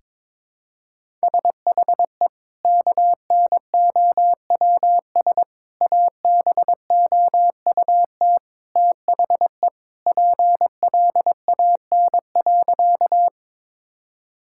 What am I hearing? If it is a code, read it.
Morse code, heard as a keyed tone, about 22 wpm, 700 Hz. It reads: SHE KNOWS ABOUT THE PLAN.
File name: practice / 22wpm / Sentences with 2-Word Phrases - 22wpm